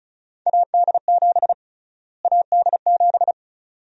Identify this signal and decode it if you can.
Morse code, heard as a keyed tone, about 34 wpm, 700 Hz. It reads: AD7 AD7
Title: USA Call Sign Prefixes - Repeated 2x - 35wpm